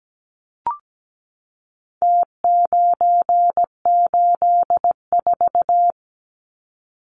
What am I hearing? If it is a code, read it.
Morse code, heard as a keyed tone, about 17 wpm, 700 Hz. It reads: T984
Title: Numbers - Contest Style - 4-Digit 17wpm